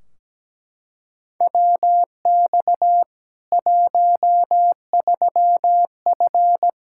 Morse code, heard as a keyed tone, about 17 wpm, 700 Hz. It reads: WX 13F
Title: K1USN SST QSO Practice 17wpm